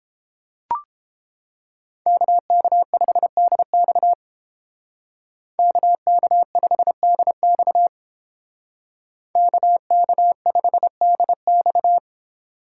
Morse code, heard as a keyed tone, about 31 wpm, 700 Hz. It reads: KK5DX KK5DX KK5DX